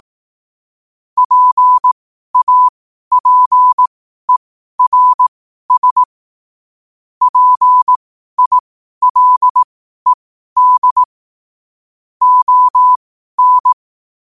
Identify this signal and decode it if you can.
Morse code, heard as a keyed tone, about 18 wpm, spaced as for 12 wpm, 1000 Hz. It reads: PAPERS PILED ON
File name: qotdfiles / 12wpm